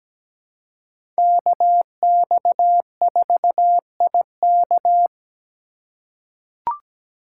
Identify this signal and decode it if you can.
Morse code, heard as a keyed tone, about 17 wpm, 700 Hz. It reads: KX4IK
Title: USA Call Signs - 5 Characters - 2x Character Spacing - 17wpm